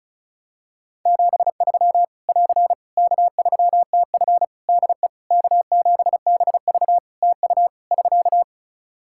Morse code, heard as a keyed tone, about 34 wpm, 700 Hz. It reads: 73 <AR> K3TF DE K7BV TU <SK>
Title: Practice QSO - Exchange 4 - Ending - 35wpm